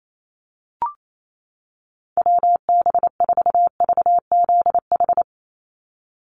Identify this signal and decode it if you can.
Morse code, heard as a keyed tone, about 28 wpm, 700 Hz. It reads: WB4VZH
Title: USA Call Signs 28wpm